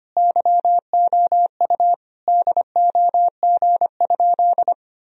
Morse code, heard as a keyed tone, about 25 wpm, 700 Hz. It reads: YOU DOG?
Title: Sentences from Top 1000 Words - Review Part 2 - 25wpm